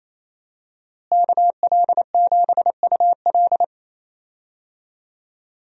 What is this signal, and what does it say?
Morse code, heard as a keyed tone, about 28 wpm, 700 Hz. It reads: KL7UL
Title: USA Call Signs - 5 Characters - Repeated 3x - 28wpm